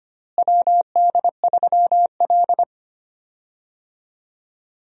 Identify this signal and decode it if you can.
Morse code, heard as a keyed tone, about 25 wpm, 700 Hz. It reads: WD3L
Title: USA Call Signs - 4 Characters - Repeated 2x - 25wpm